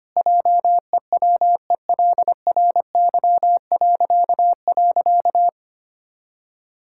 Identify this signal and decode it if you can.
Morse code, heard as a keyed tone, about 25 wpm, 700 Hz. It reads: JEWELRY..
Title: Pangram Challenge v2 - 25wpm